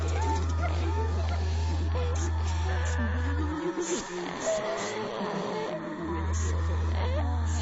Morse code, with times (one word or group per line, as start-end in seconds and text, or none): none